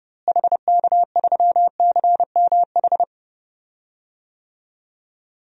Morse code, heard as a keyed tone, about 30 wpm, 700 Hz. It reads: HK3CMH
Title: DXCC-100 Call Signs - 6 Characters - Repeated 3x - 30wpm